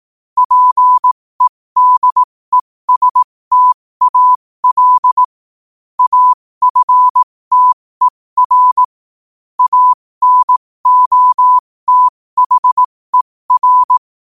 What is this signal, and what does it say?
Morse code, heard as a keyed tone, about 18 wpm, spaced as for 15 wpm, 1000 Hz. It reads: PEDESTAL AFTER ANOTHER
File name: qotdfiles / 15wpm